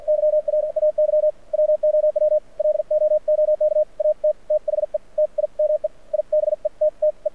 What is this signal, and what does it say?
Morse code, heard as a keyed tone, about 54 wpm, spaced as for 44 wpm, 600 Hz. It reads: YR AY WOW LOOK AT THE TIME I BETTE
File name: archived / CRSnet